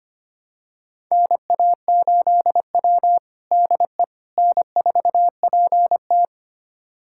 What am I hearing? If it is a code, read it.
Morse code, heard as a keyed tone, about 25 wpm, 700 Hz. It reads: NA8W DE N4PT
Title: USA Callsign DE Callsign - 4 Characters - 25wpm